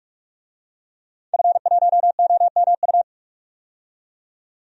Morse code, heard as a keyed tone, about 46 wpm, 700 Hz. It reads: F1OMU
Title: DXCC-100 Call Signs - 5 Characters - 45wpm